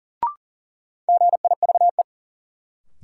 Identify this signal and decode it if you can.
Morse code, heard as a keyed tone, about 40 wpm, 700 Hz. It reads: GIVE
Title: Sentences from Top 200 Words - New Word Drill - 40wpm